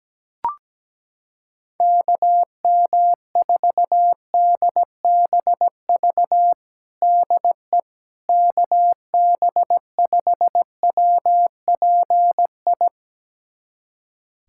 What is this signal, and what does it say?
Morse code, heard as a keyed tone, about 17 wpm, 700 Hz. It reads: KM4DBV DE KB5WPI
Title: USA Callsign DE Callsign - 6 Characters - 17wpm